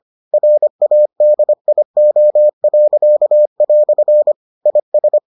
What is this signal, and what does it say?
Morse code, heard as a keyed tone, about 25 wpm, 600 Hz. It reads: RADIO." IS